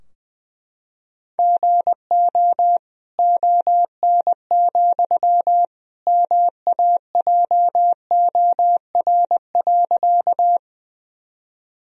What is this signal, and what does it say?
Morse code, heard as a keyed tone, about 20 wpm, 700 Hz. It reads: GO ON, MAJOR.
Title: Sentences from Top 1000 Words - Review Part 4 - 20wpm